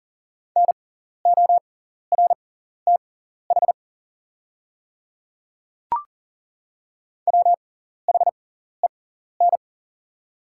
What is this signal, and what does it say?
Morse code, heard as a keyed tone, about 38 wpm, spaced as for 14 wpm, 700 Hz. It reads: NORTH WHEN
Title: 6x Character Spacing - Top 300 Words - 40wpm